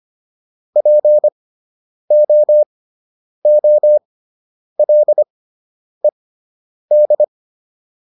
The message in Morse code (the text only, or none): POOLED